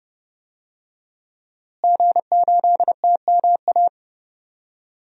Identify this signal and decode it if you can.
Morse code, heard as a keyed tone, about 30 wpm, 700 Hz. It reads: G8TMA
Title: DXCC-100 Call Signs - 5 Characters - Repeated 2x - 30wpm